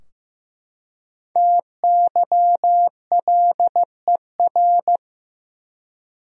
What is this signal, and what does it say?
Morse code, heard as a keyed tone, about 15 wpm, 700 Hz. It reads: TYLER
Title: USA Common Names - 15wpm